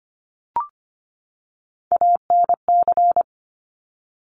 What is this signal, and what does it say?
Morse code, heard as a keyed tone, about 25 wpm, 700 Hz. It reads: ANC
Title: Top Three Letter Combinations in Words 25wpm